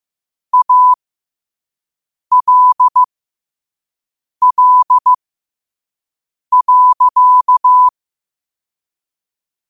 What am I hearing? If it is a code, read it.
Morse code, heard as a keyed tone, about 15 wpm, spaced as for 5 wpm, 1000 Hz. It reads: ALL.
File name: qotdfiles / 5wpm